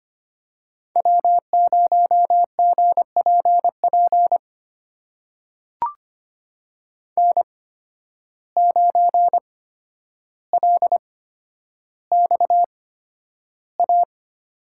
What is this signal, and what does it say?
Morse code, heard as a keyed tone, about 25 wpm, spaced as for 11 wpm, 700 Hz. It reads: W0GPP N9LXA
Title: USA Call Signs - 5 Characters - 8x Character Spacing - 25wpm